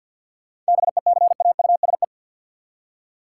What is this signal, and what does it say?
Morse code, heard as a keyed tone, about 52 wpm, 700 Hz. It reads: BECAUSE
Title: Seven Letter Words - 50wpm